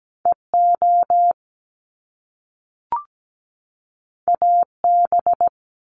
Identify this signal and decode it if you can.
Morse code, heard as a keyed tone, about 17 wpm, 700 Hz. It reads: EO AB
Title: Top Two Letter Combinations in Words 17wpm